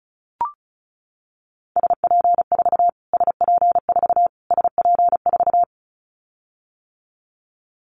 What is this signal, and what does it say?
Morse code, heard as a keyed tone, about 34 wpm, 700 Hz. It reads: SP4 SP4 SP4
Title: Callsign Prefixes for Top 25 DXCC - Repeated 3x - 35wpm